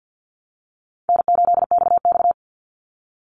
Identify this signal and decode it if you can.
Morse code, heard as a keyed tone, about 49 wpm, 700 Hz. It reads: N8XX
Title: Easy Call Signs Repeated 2x - 50wpm